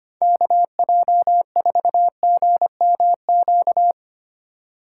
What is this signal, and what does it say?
Morse code, heard as a keyed tone, about 32 wpm, 700 Hz. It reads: KJ4GMQ